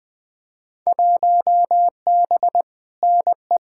Morse code, heard as a keyed tone, about 20 wpm, 700 Hz. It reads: 1B NE
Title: Field Day Exchanges Encore - 20wpm